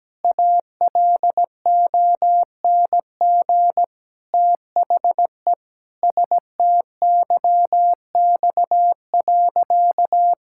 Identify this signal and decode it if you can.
Morse code, heard as a keyed tone, about 17 wpm, 700 Hz. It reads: ALONG THE STYX.
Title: Pangram Challenge v2 - 17wpm